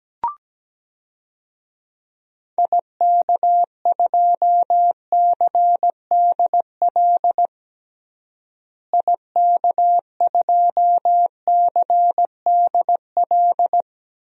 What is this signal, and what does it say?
Morse code, heard as a keyed tone, about 17 wpm, 700 Hz. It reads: IK2CDL IK2CDL
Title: DXCC-100 Call Signs - 6 Characters - Repeated 3x - 17wpm